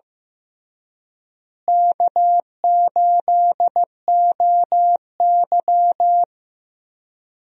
Morse code, heard as a keyed tone, about 15 wpm, 700 Hz. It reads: K8OY
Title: USA Call Signs - 4 Characters - Repeated 3x - 15wpm